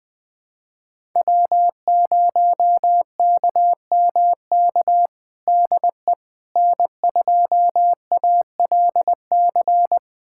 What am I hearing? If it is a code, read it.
Morse code, heard as a keyed tone, about 21 wpm, 700 Hz. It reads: W0KMK DE N2ALC
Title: USA Callsign DE Callsign - 5 Characters - 20wpm